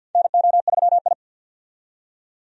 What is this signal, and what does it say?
Morse code, heard as a keyed tone, about 51 wpm, 700 Hz. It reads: NO3I
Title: USA Call Signs - 4 Characters - 50wpm